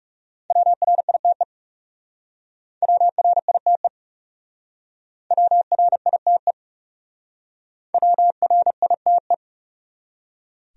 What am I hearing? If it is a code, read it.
Morse code, heard as a keyed tone, about 38 wpm, 700 Hz. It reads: WRITE WRITE WRITE WRITE